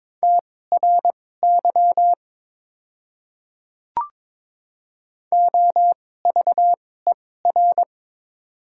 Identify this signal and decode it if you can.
Morse code, heard as a keyed tone, about 21 wpm, spaced as for 16 wpm, 700 Hz. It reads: TRY OVER
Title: 2x Character Spacing - Top 300 Words - 22wpm